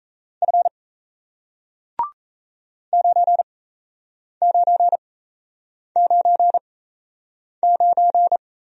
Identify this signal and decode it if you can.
Morse code, heard as a keyed tone, about 38 wpm, spaced as for 8 wpm, 700 Hz. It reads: F 9999